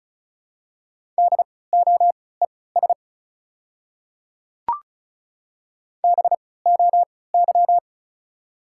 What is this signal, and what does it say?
Morse code, heard as a keyed tone, about 36 wpm, spaced as for 20 wpm, 700 Hz. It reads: DOES BOY
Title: 3x Character Spacing - Top 200 Words - 35wpm